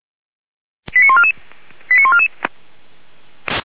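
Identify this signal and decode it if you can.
Five-tone selective call (ZVEI), heard as quick runs of five tones, tones of about 70 ms. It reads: 89144 89144
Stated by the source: Five-tone ZVEI selective call 89155 received by SDR.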